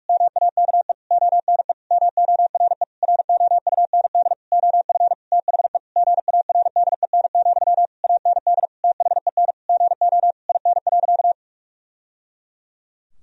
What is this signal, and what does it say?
Morse code, heard as a keyed tone, about 42 wpm, 700 Hz. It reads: MAKE ONE MORE ROUND OF THE GARDEN, AND THEN GO IN.
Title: Sentences from Top 1000 Words - Review Part 4 - 45wpm